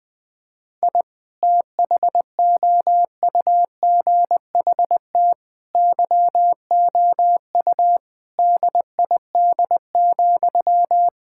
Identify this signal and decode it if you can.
Morse code, heard as a keyed tone, about 20 wpm, 700 Hz. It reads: I THOUGHT YOU DID,
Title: Sentences from Top 1000 Words - Review Part 5 - 20wpm